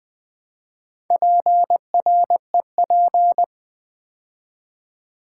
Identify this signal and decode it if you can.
Morse code, heard as a keyed tone, about 20 wpm, 700 Hz. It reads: PREP
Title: Build A Word - 4 Letters At A Time - 11-Letter Words - 20wpm